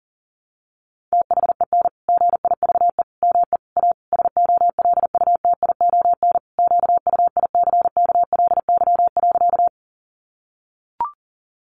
Morse code, heard as a keyed tone, about 40 wpm, 700 Hz. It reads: THEN GIVE ME A SOLUTION QUICKLY.
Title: Sentences from Top 1300 Words 40wpm